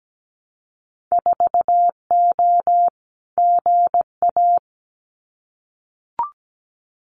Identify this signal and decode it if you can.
Morse code, heard as a keyed tone, about 17 wpm, 700 Hz. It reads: 4O GA
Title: Winter Field Day Exchanges 17wpm